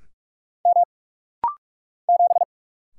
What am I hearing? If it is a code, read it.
Morse code, heard as a keyed tone, about 44 wpm, 700 Hz. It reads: M 7
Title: Single Letter-Number Alt - Rapid-Fire 45wpm